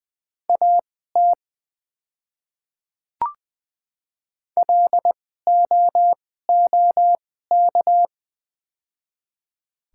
Morse code, heard as a keyed tone, about 20 wpm, spaced as for 15 wpm, 690 Hz. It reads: AT LOOK
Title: 2x Character Spacing - Top 200 Words - 20wpm